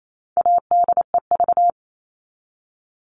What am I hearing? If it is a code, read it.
Morse code, heard as a keyed tone, about 28 wpm, 700 Hz. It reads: ADEV
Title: All Cut Numbers - 4-Digit 28wpm